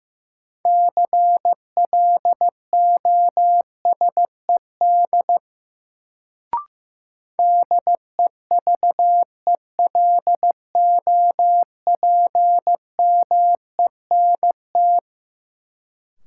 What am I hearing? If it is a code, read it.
Morse code, heard as a keyed tone, about 16 wpm, 700 Hz. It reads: CLOSED DEVELOPMENT
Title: Sentences from Top 1400 Words - New Word Drill Alt - 15wpm